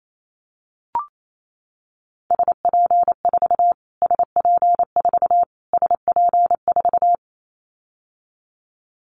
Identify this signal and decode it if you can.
Morse code, heard as a keyed tone, about 28 wpm, 700 Hz. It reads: SP4 SP4 SP4
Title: Callsign Prefixes for Top 25 DXCC - Repeated 3x - 28wpm